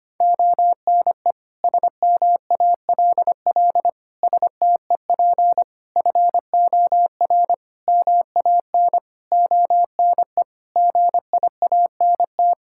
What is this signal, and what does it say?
Morse code, heard as a keyed tone, about 25 wpm, 700 Hz. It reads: ONE SMALL STEP FOR MAN ONE GIANT
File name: Sets of 9 Words - 25wpm